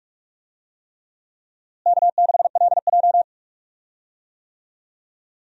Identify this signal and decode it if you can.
Morse code, heard as a keyed tone, about 46 wpm, 700 Hz. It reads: K6LJ
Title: USA Call Signs - 4 Characters - Repeated 3x - 45wpm